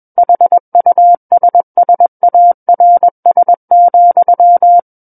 Morse code, heard as a keyed tone, about 21 wpm, 700 Hz. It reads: HUSSARS,
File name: Book One / 12-Chapter 12 - 21 WPM